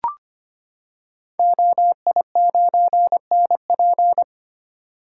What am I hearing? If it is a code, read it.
Morse code, heard as a keyed tone, about 25 wpm, 700 Hz. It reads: OI9NP
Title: Code Groups - 5 Letter:Numbers - 25wpm